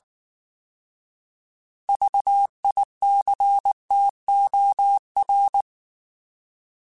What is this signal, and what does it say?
Morse code, heard as a keyed tone, about 19 wpm, 800 Hz. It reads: VICTOR